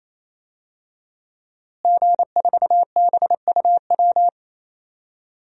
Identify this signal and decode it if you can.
Morse code, heard as a keyed tone, about 27 wpm, 700 Hz. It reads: G4BUW
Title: DXCC-100 Call Signs - 5 Characters - Repeated 3x - 28wpm